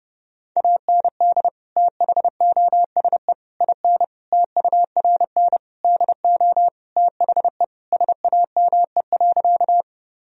Morse code, heard as a keyed tone, about 30 wpm, 700 Hz. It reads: AND THOSE IN TURN DO THE SAME.
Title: Sentences from Top 1000 Words - Review Part 1 - 30wpm